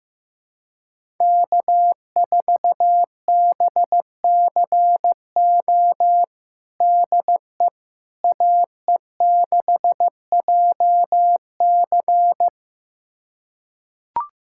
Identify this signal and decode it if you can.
Morse code, heard as a keyed tone, about 15 wpm, 700 Hz. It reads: K4BCO DE AE6JC
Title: USA Callsign DE Callsign - 5 Characters - 15wpm